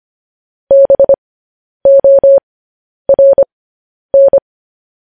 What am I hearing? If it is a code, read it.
Morse code, heard as a keyed tone, about 25 wpm, spaced as for 10 wpm, 550 Hz. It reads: BORN